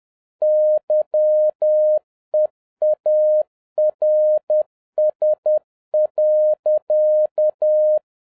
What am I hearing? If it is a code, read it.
Morse code, heard as a keyed tone, about 10 wpm, 610 Hz. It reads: YEARS.